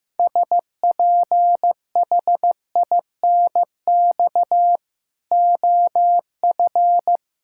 Morse code, heard as a keyed tone, about 16 wpm, 700 Hz. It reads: SPHINX OF
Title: Pangram Challenge v2 - 15wpm